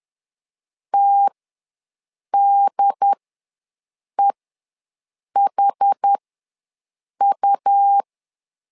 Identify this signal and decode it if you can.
Morse code, heard as a keyed tone, about 10 wpm, spaced as for 6 wpm, 800 Hz. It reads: TDEHU
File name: LEZIONI / 2 UHDN / 2 UHD ws 10wpm 2058